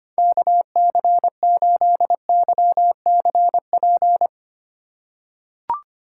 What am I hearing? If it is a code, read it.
Morse code, heard as a keyed tone, about 25 wpm, 700 Hz. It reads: KC8YCP
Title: USA Call Signs - 6 Characters - 4x Character Spacing - 25wpm